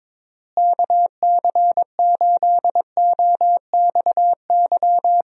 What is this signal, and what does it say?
Morse code, heard as a keyed tone, about 22 wpm, 700 Hz. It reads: KC8OXY
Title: USA Call Signs - 6 Characters - Repeated 3x - 22wpm